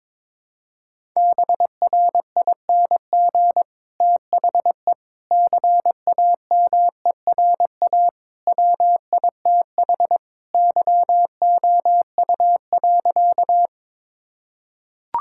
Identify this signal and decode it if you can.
Morse code, heard as a keyed tone, about 22 wpm, 700 Hz. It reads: BRING THE CAMERA WITH YOU.
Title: Sentences from Top 1000 Words - Review Part 5 - 22wpm